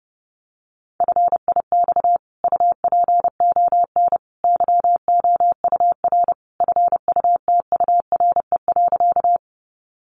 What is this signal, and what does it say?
Morse code, heard as a keyed tone, about 30 wpm, 700 Hz. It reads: FIX UPON YOUR FUTURE.
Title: Sentences from Top 1400 Words 30wpm